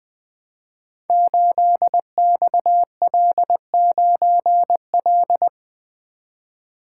Morse code, heard as a keyed tone, about 20 wpm, 700 Hz. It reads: 8XL9L
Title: Code Groups - 5 Letter:Numbers - 20wpm